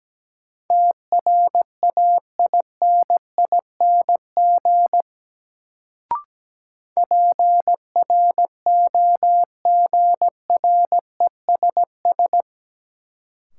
Morse code, heard as a keyed tone, about 17 wpm, 700 Hz. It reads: TRAINING PROGRESS
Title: Sentences from Top 1400 Words - New Word Drill Alt - 17wpm